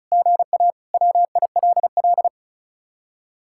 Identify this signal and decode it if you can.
Morse code, heard as a keyed tone, about 44 wpm, 700 Hz. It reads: GA WILL